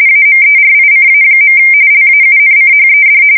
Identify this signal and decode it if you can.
RTTY, about 45.45 baud. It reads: AMERICA. LAUNCHED
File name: fsk170 / battleshipnj